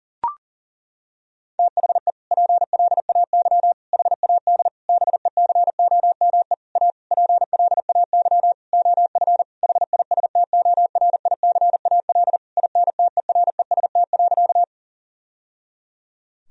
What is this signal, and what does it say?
Morse code, heard as a keyed tone, about 40 wpm, 700 Hz. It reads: THE PLAY HAD BECOME A PLAY OF HISTORICAL INTEREST.
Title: Sentences from Top 1000 Words - Review Part 4 - 40wpm